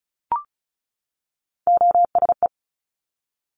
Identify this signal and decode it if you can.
Morse code, heard as a keyed tone, about 36 wpm, 700 Hz. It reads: OSE
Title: Top Three Letter Combinations in Words 35wpm